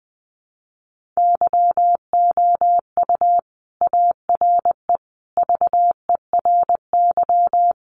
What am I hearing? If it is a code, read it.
Morse code, heard as a keyed tone, about 20 wpm, 700 Hz. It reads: YOU ARE VERY
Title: Sentences from Top 1400 Words 20wpm